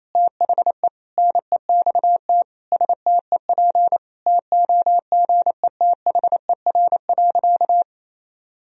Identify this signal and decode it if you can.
Morse code, heard as a keyed tone, about 28 wpm, 700 Hz. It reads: THE NEXT STEP TOGETHER.
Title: Sentences from Top 1000 Words - Review Part 2 - 28wpm